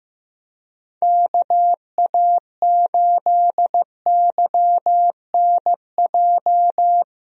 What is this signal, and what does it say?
Morse code, heard as a keyed tone, about 15 wpm, 700 Hz. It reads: KA8YNJ
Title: USA Call Signs - 6 Characters - 15wpm